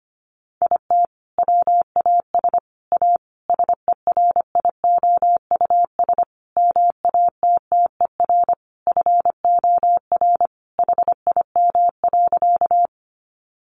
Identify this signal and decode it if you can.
Morse code, heard as a keyed tone, about 25 wpm, 700 Hz. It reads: IT WAS A SERIOUS MATTER FOR HIM.
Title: Sentences from Top 1100 Words 25wpm